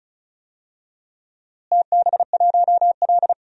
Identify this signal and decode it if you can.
Morse code, heard as a keyed tone, about 34 wpm, 700 Hz. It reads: TB1L
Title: DXCC-100 Call Signs - 4 Characters - Repeated 2x - 35wpm